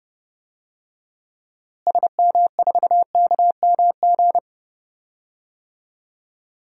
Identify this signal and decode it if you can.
Morse code, heard as a keyed tone, about 30 wpm, 700 Hz. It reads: SM4KMG
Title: DXCC-100 Call Signs - 6 Characters - Repeated 3x - 30wpm